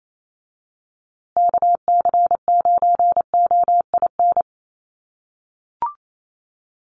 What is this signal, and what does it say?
Morse code, heard as a keyed tone, about 28 wpm, 700 Hz. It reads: KC9OIN
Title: USA Call Signs Repeated 2x 28wpm